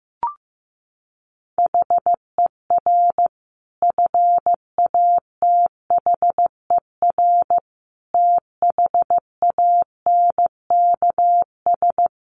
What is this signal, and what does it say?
Morse code, heard as a keyed tone, about 15 wpm, 700 Hz. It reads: HER FATHER THANKS
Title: Sentences from Top 600 Words 15wpm